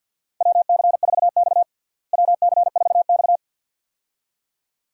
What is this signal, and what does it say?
Morse code, heard as a keyed tone, about 46 wpm, 700 Hz. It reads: WC4X WC4X
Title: USA Call Signs - 4 Characters - Repeated 2x - 50wpm